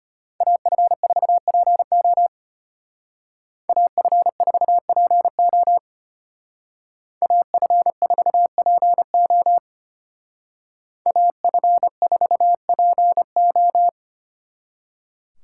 Morse code, heard as a keyed tone, about 33 wpm, 700 Hz. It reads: AF4PO AF4PO AF4PO AF4PO